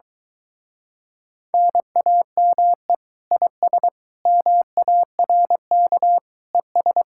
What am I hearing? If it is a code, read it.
Morse code, heard as a keyed tone, about 24 wpm, 700 Hz. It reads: NAME IS MARK ES